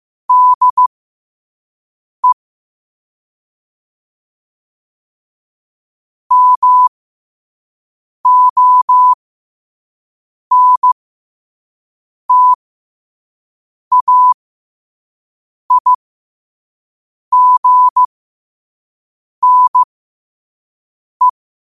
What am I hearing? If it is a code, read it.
Morse code, heard as a keyed tone, about 15 wpm, spaced as for 5 wpm, 1000 Hz. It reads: DE MONTAIGNE
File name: qotdfiles / 5wpm